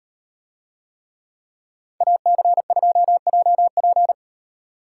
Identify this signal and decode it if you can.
Morse code, heard as a keyed tone, about 35 wpm, 700 Hz. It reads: AC2JP